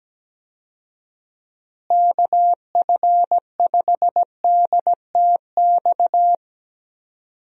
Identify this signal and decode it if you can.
Morse code, heard as a keyed tone, about 17 wpm, 700 Hz. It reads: KF5DTX
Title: USA Call Signs - 6 Characters - Repeated 3x - 17wpm